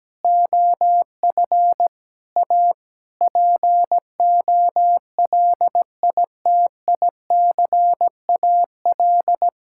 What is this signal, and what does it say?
Morse code, heard as a keyed tone, about 17 wpm, 700 Hz. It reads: OF A POLITICAL
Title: Sentences from Top 1000 Words - Review Part 5 - 17wpm